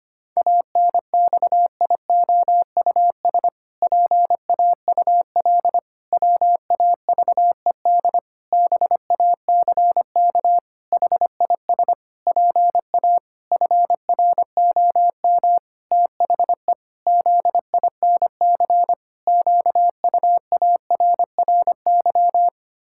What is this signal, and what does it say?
Morse code, heard as a keyed tone, about 25 wpm, 700 Hz. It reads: ANXIOUS PAUL WAVED BACK HIS PA FROM THE ZINC QUARRY
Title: Pangram Challenge v2 - 25wpm